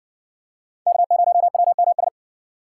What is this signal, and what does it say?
Morse code, heard as a keyed tone, about 52 wpm, 700 Hz. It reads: K0WMS